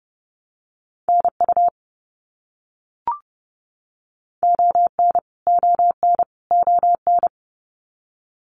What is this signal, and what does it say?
Morse code, heard as a keyed tone, about 30 wpm, 700 Hz. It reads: NU ON ON ON
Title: Canadian Provinces and Territories - Repeated 3x - 30wpm